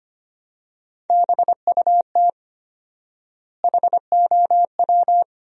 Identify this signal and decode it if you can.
Morse code, heard as a keyed tone, about 25 wpm, 700 Hz. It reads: BUT HOW
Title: 4x Word-Spacing - Sentences from Top 100 Words - 25wpm